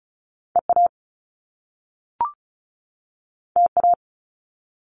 Morse code, heard as a keyed tone, about 36 wpm, 700 Hz. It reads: EA TA
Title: Top Two Letter Combinations in Words 35wpm